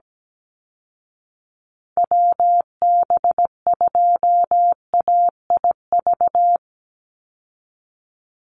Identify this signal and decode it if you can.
Morse code, heard as a keyed tone, about 17 wpm, 700 Hz. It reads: WB2AIV
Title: Call Signs Repeated 2x - 17wpm